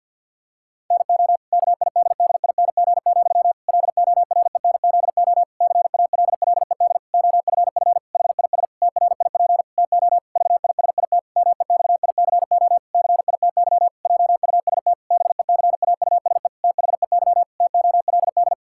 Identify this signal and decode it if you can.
Morse code, heard as a keyed tone, about 50 wpm, 700 Hz. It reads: NO KIDDING, LORENZO CALLED OFF HIS TRIP TO VISIT MEXICO CITY JUST BECAUSE THEY TOLD
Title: Pangram Challenge v2 - 50wpm